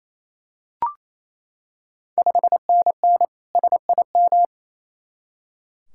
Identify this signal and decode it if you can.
Morse code, heard as a keyed tone, about 28 wpm, 700 Hz. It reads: 5NN SIM
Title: Ontario QSO Party Exchange 28wpm